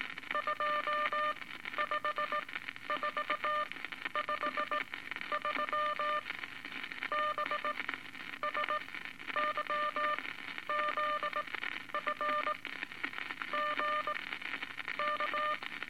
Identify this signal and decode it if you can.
Morse code, heard as a keyed tone, about 18 wpm, spaced as for 11 wpm, 1200 Hz, with noise. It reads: 25453 BSYZF GK
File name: Eight Words Per Minute